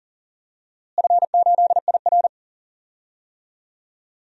Morse code, heard as a keyed tone, about 40 wpm, 700 Hz. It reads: F8IR
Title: DXCC-100 Call Signs - 4 Characters - Repeated 2x - 40wpm